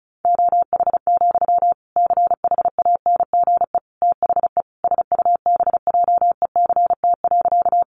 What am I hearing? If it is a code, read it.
Morse code, heard as a keyed tone, about 34 wpm, 700 Hz. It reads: OH, CHANGE THE SUBJECT.
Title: Sentences from Top 800 Words 35wpm